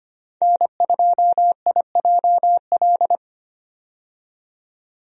Morse code, heard as a keyed tone, about 25 wpm, 700 Hz. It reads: N2IJL
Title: USA Call Signs - 5 Characters - 25wpm